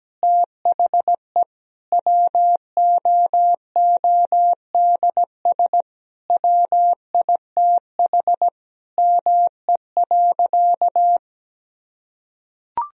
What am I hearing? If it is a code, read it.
Morse code, heard as a keyed tone, about 17 wpm, 700 Hz. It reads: THE WOODS WITH ME.
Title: Sentences from Top 1000 Words - Review Part 2 - 17wpm